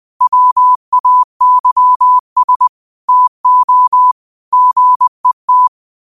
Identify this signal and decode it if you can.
Morse code, heard as a keyed tone, about 20 wpm, 1000 Hz. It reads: WAYS TO GET
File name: qotdfiles / 20wpm